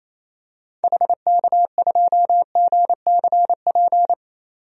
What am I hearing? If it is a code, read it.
Morse code, heard as a keyed tone, about 28 wpm, 700 Hz. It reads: HK2GCP
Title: DXCC-100 Call Signs - 6 Characters - 28wpm